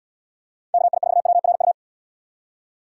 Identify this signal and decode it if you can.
Morse code, heard as a keyed tone, about 72 wpm, 700 Hz. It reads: KI4WRU